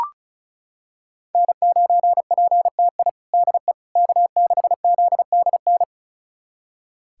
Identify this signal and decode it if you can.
Morse code, heard as a keyed tone, about 34 wpm, 700 Hz. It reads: N9PTI DE K6ZDN
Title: USA Callsign DE Callsign - 5 Characters - 35wpm